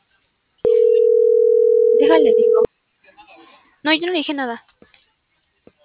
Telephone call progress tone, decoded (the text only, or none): ringback tone